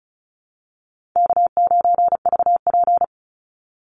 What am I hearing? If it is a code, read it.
Morse code, heard as a keyed tone, about 34 wpm, 700 Hz. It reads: K9VP
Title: Call Signs - 35wpm